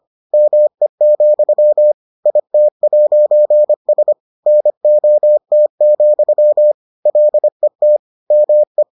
Morse code, heard as a keyed tone, about 25 wpm, 600 Hz. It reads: ME, IT'S NOT, LET ME